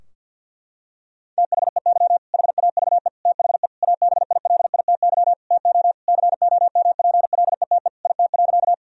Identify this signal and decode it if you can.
Morse code, heard as a keyed tone, about 50 wpm, 700 Hz. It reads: THEY HAVE THE ABILITY TO COMPLETE IT.
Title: Sentences from Top 1000 Words - Review Part 3 - 50wpm